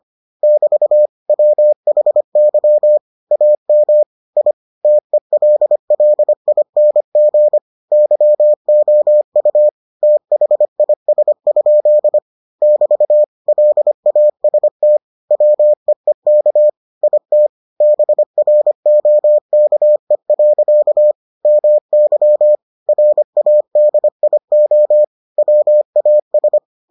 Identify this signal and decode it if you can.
Morse code, heard as a keyed tone, about 25 wpm, 600 Hz. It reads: <BT> WHY AM I TELLING YOU THIS? <BT> LAST WEEK IT BROKE. MY RADIO WAS